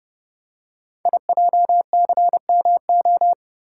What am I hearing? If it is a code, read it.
Morse code, heard as a keyed tone, about 29 wpm, 690 Hz. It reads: IJCMO
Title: Code Groups - 5 Letters - 30wpm